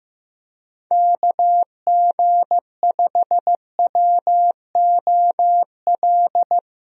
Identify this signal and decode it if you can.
Morse code, heard as a keyed tone, about 15 wpm, 700 Hz. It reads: KG5WOL
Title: USA Call Signs - 6 Characters - Repeated 3x - 15wpm